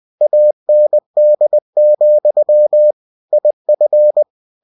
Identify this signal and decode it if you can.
Morse code, heard as a keyed tone, about 20 wpm, 600 Hz. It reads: AND, IF
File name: cw-books / winnie-the-pooh / winnie-20wpm-page-0024